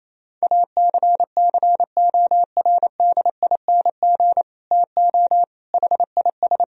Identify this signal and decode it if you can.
Morse code, heard as a keyed tone, about 28 wpm, 700 Hz. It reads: ACCORDING TO HIS
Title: Sentences from Top 1000 Words - Review Part 4 - 28wpm